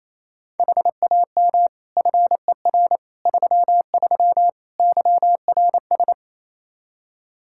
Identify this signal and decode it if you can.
Morse code, heard as a keyed tone, about 27 wpm, 700 Hz. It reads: HAM FER 33 YRS
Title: Practice QSO - Exchange 3 - More Info - 28wpm